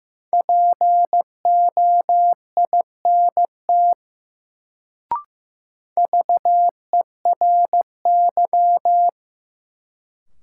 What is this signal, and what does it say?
Morse code, heard as a keyed tone, about 15 wpm, 700 Hz. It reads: POINT VERY
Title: Sentences from Top 200 Words - New Word Drill - 15wpm